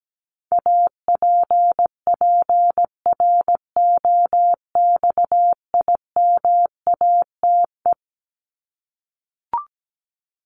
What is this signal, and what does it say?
Morse code, heard as a keyed tone, about 17 wpm, 700 Hz. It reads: APPROXIMATE
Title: LY Suffix - Top 100 Words with Base Words 17wpm